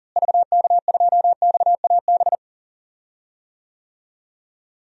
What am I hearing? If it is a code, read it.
Morse code, heard as a keyed tone, about 40 wpm, 700 Hz. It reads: VK2XAB
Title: DXCC-100 Call Signs - 6 Characters - Repeated 2x - 40wpm